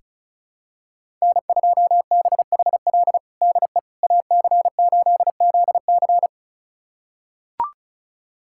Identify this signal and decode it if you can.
Morse code, heard as a keyed tone, about 34 wpm, 700 Hz. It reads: N2BHL DE AC8ZC
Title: USA Callsign DE Callsign - 5 Characters - 35wpm